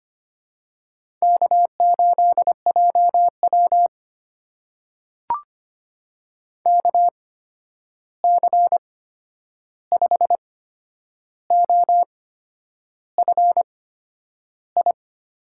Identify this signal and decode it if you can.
Morse code, heard as a keyed tone, about 25 wpm, spaced as for 9 wpm, 700 Hz. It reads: K8JW KC5OFI
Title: USA Call Signs - 8x Character Spacing - 25wpm